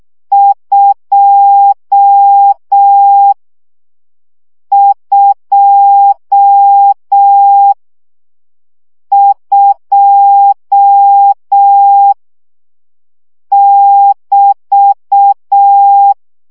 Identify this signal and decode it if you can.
Morse code, heard as a keyed tone, about 6 wpm, 800 Hz. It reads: 2 2 2 <BT>
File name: CW gyakorlas 2 5 baud